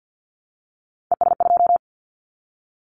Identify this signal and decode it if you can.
Morse code, heard as a keyed tone, about 50 wpm, 700 Hz. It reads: ES2
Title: Callsign Prefixes for Top 100 DXCC - 50wpm